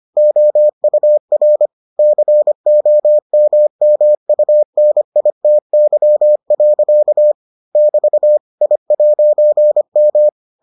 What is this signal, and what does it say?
Morse code, heard as a keyed tone, about 25 wpm, 600 Hz. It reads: OUR COMMUNITY. <BT> I'M